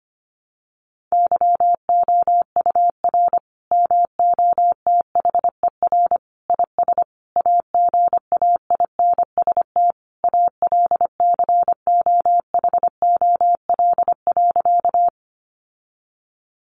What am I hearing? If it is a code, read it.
Morse code, heard as a keyed tone, about 25 wpm, 700 Hz. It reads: YOUR MOTHER IS AGAINST ALCOHOL.
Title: Sentences from Top 1400 Words 25wpm